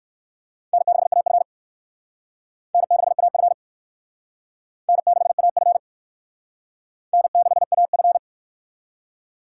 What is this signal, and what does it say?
Morse code, heard as a keyed tone, about 60 wpm, 700 Hz. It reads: N6AF N6AF N6AF N6AF